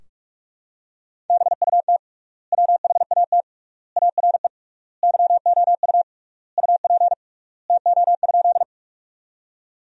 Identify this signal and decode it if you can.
Morse code, heard as a keyed tone, about 41 wpm, 700 Hz. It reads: BUT WHAT ARE YOU UP TO?
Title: 3x Word-Spacing - Sentences from Top 100 Words - 45wpm